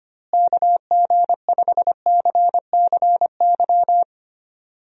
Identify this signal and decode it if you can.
Morse code, heard as a keyed tone, about 25 wpm, 700 Hz. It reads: KG5CCY
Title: USA Call Signs - 6 Characters - Repeated 2x - 25wpm